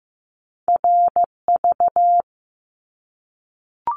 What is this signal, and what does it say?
Morse code, heard as a keyed tone, about 15 wpm, 700 Hz. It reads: RV
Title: Top Two Letter Combinations in Words 15wpm